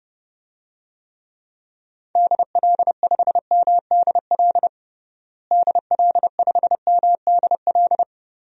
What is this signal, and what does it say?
Morse code, heard as a keyed tone, about 30 wpm, 700 Hz. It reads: DL5MDL DL5MDL
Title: DXCC-100 Call Signs - 6 Characters - Repeated 2x - 30wpm